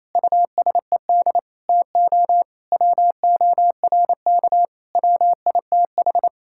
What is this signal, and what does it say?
Morse code, heard as a keyed tone, about 27 wpm, 700 Hz. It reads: USED TO WORK WITH
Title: Sentences from Top 1000 Words - Review Part 6 - 28wpm